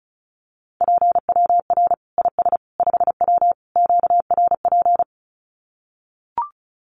Morse code, heard as a keyed tone, about 34 wpm, 700 Hz. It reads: PWR IS 5W QRP
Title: Common QSO Phrases 35wpm